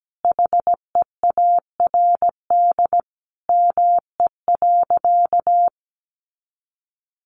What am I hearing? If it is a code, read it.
Morse code, heard as a keyed tone, about 17 wpm, 700 Hz. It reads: HEARD ME.
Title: Sentences from Top 400 Words 17wpm